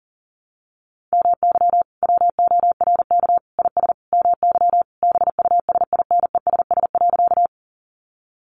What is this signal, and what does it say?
Morse code, heard as a keyed tone, about 40 wpm, 700 Hz. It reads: MY WORK IS MY BUSINESS.
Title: Sentences from Top 600 Words 40wpm